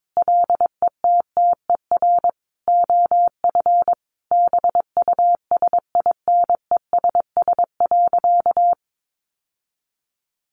Morse code, heard as a keyed tone, about 22 wpm, 700 Hz. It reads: LETTER OF BUSINESS.
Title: Sentences from Top 1400 Words 22wpm